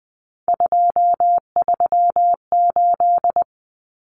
Morse code, heard as a keyed tone, about 20 wpm, 700 Hz. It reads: 238
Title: Numbers - 3-Digit 20wpm